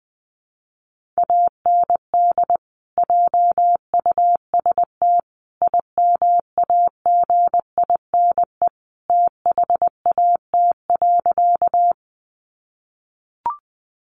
Morse code, heard as a keyed tone, about 20 wpm, 700 Hz. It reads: AND JUST IMAGINE THAT.
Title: Sentences from Top 1400 Words 20wpm